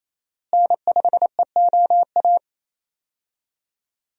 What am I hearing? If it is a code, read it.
Morse code, heard as a keyed tone, about 28 wpm, 700 Hz. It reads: N5EOA
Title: USA Call Signs - 5 Characters - Repeated 2x - 28wpm